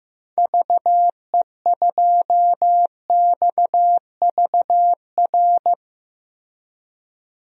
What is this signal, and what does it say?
Morse code, heard as a keyed tone, about 15 wpm, 700 Hz. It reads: VE2XVR
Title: DXCC-100 Call Signs - 6 Characters - 15wpm